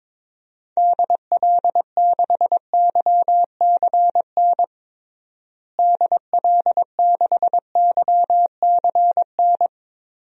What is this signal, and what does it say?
Morse code, heard as a keyed tone, about 22 wpm, 700 Hz. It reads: DL6YCN DL6YCN
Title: DXCC-100 Call Signs - 6 Characters - Repeated 3x - 22wpm